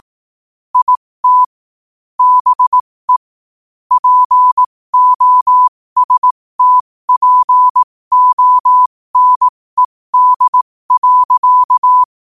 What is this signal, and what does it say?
Morse code, heard as a keyed tone, about 18 wpm, spaced as for 15 wpm, 1000 Hz. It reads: IT BE POSTPONED.
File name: qotdfiles / 15wpm